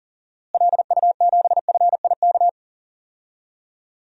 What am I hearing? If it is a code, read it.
Morse code, heard as a keyed tone, about 40 wpm, 700 Hz. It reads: LU7FIK
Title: DXCC-100 Call Signs - 6 Characters - Repeated 2x - 40wpm